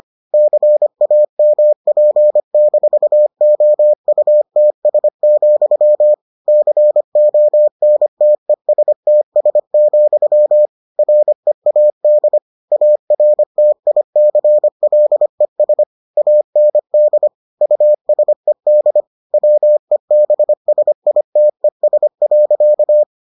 Morse code, heard as a keyed tone, about 25 wpm, 600 Hz. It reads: CAMP-OUTS, CONTESTS, READ ARTICLES AND USED WEBSITES.